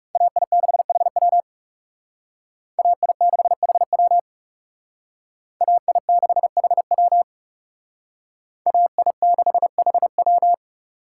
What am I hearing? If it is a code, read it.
Morse code, heard as a keyed tone, about 36 wpm, 700 Hz. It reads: AI6HW AI6HW AI6HW AI6HW
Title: Speed Racing - USA 2x2 Call Signs 30wpm